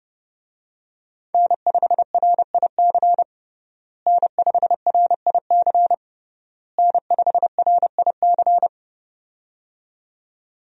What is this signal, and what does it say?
Morse code, heard as a keyed tone, about 30 wpm, 700 Hz. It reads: N5RIC N5RIC N5RIC
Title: USA Call Signs - 5 Characters - Repeated 3x - 30wpm